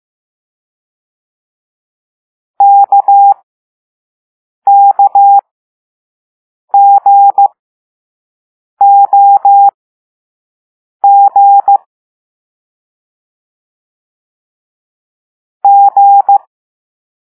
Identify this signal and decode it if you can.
Morse code, heard as a keyed tone, about 15 wpm, spaced as for 5 wpm, 800 Hz. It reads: KKGOG G
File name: Morse / Level4A